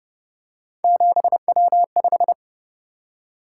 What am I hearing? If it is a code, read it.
Morse code, heard as a keyed tone, about 29 wpm, 700 Hz. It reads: 7W5